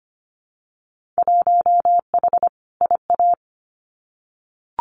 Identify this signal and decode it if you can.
Morse code, heard as a keyed tone, about 25 wpm, 700 Hz. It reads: 1H IA
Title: Winter Field Day Exchanges 25wpm